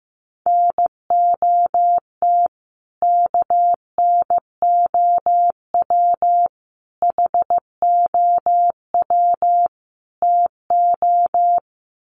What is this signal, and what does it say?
Morse code, heard as a keyed tone, about 15 wpm, 700 Hz. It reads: NOT KNOW HOW TO
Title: Sets of 4 Words 15wpm